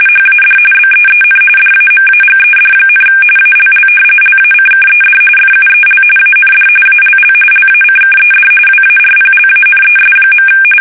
RTTY, about 45.45 baud. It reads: HAD BEEN DAMAGED, ALL LACKED FUEL AND ALL LACKED WARPLANES.
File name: fsk850